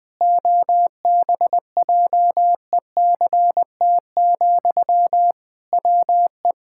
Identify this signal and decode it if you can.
Morse code, heard as a keyed tone, about 20 wpm, 700 Hz. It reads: OBJECT, WE
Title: Sentences from Top 1000 Words - Review Part 6 - 20wpm